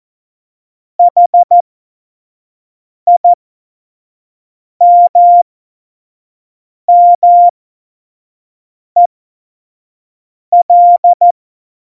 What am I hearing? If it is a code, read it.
Morse code, heard as a keyed tone, about 14 wpm, spaced as for 5 wpm, 700 Hz. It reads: HIMMEL
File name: Morsekurs / Witze / Witz 04. September 2018, 25 F70